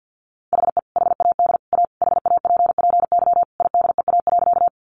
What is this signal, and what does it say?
Morse code, heard as a keyed tone, about 50 wpm, 700 Hz. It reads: HE HAD A HAPPY IDEA.
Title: Sentences from Top 1000 Words 50wpm